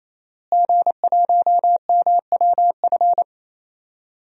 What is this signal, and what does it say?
Morse code, heard as a keyed tone, about 28 wpm, 700 Hz. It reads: G1MWF
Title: DXCC-100 Call Signs - 5 Characters - Repeated 3x - 28wpm